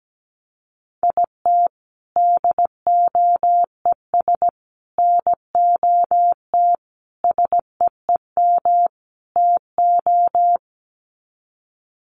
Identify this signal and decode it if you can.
Morse code, heard as a keyed tone, about 17 wpm, 700 Hz. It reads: IT DOES NOT SEEM TO
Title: Sets of 5 Words Encore2 17wpm